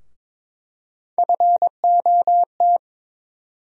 Morse code, heard as a keyed tone, about 22 wpm, 700 Hz. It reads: FOT
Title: USA Call Sign Postfixes - 3 Letters 22wpm